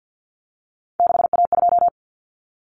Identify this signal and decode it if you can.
Morse code, heard as a keyed tone, about 50 wpm, 700 Hz. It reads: 6A2
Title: Common Cut Numbers - 3-Digit 50wpm